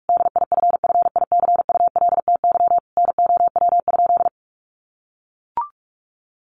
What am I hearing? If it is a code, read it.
Morse code, heard as a keyed tone, about 44 wpm, 700 Hz. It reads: DIFFICULTY NOW?
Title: Sentences from Top 1500 Words 45wpm